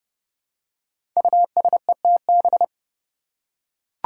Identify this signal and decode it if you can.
Morse code, heard as a keyed tone, about 30 wpm, 700 Hz. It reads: USETB
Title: Five Challenging Letter-Number Combinations 30wpm